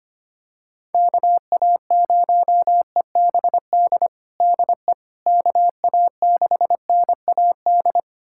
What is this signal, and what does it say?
Morse code, heard as a keyed tone, about 25 wpm, 700 Hz. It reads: KA0EBD DE KA6NAD
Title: USA Callsign DE Callsign - 6 Characters - 25wpm